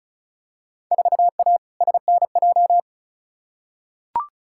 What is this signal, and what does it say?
Morse code, heard as a keyed tone, about 36 wpm, 700 Hz. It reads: 4A SNJ
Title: Field Day Exchanges Encore - 35wpm